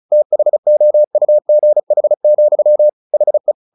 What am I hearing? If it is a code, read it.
Morse code, heard as a keyed tone, about 34 wpm, 600 Hz. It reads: THOUGH, HE
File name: the old man and the sea / the old man and the sea 1 35wpm